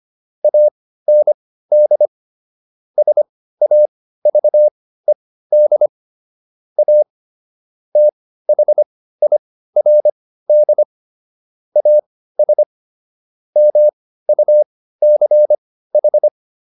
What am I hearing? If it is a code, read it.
Morse code, heard as a keyed tone, about 25 wpm, spaced as for 15 wpm, 600 Hz. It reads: AND SAVED A THIRD AS MUCH